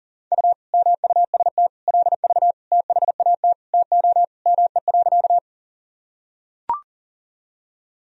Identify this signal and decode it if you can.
Morse code, heard as a keyed tone, about 40 wpm, 700 Hz. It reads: U MUST LV THAT TO ME.
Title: Sentences from Top 400 Words and Abbreviations 40wpm